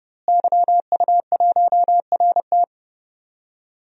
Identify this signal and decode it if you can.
Morse code, heard as a keyed tone, about 30 wpm, 700 Hz. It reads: YU1RT
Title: DXCC-100 Call Signs - 5 Characters - Repeated 3x - 30wpm